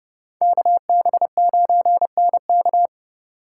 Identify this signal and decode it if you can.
Morse code, heard as a keyed tone, about 37 wpm, 700 Hz. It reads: KB9NK
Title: Speed Racing - USA 2x2 Call Signs 30wpm